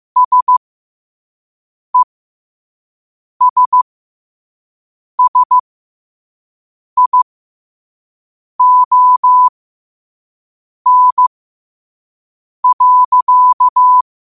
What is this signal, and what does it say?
Morse code, heard as a keyed tone, about 15 wpm, spaced as for 5 wpm, 1000 Hz. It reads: SESSION.
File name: qotdfiles / 5wpm